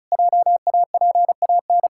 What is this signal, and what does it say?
Morse code, heard as a keyed tone, about 36 wpm, 700 Hz. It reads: JAPAN
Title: Pangram Challenge v2 - 35wpm